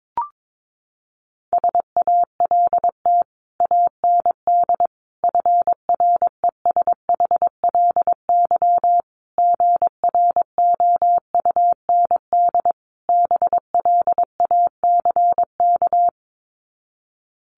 Morse code, heard as a keyed tone, about 22 wpm, 700 Hz. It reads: SALT AND FRESHLY GROUND BLACK
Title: Sets of 5 Words Encore 22wpm